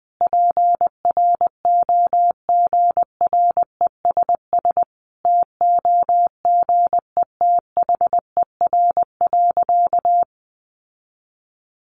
Morse code, heard as a keyed tone, about 20 wpm, 700 Hz. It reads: PROGRESS TOGETHER.
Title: Sentences from Top 1400 Words 20wpm